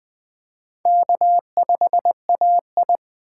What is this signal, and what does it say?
Morse code, heard as a keyed tone, about 20 wpm, 700 Hz. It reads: K5AI
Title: USA Call Signs - 4 Characters - 2x Character Spacing - 20wpm